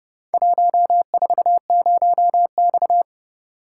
Morse code, heard as a keyed tone, about 29 wpm, 700 Hz. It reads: 140X